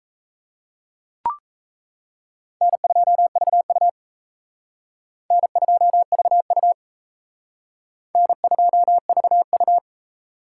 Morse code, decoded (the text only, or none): N2VU N2VU N2VU